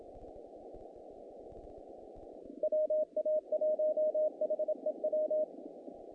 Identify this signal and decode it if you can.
Morse code, heard as a keyed tone, about 27 wpm, 620 Hz. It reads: WA1HEW
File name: contests / naqp-cw-k5tr-op-k5pi- -2100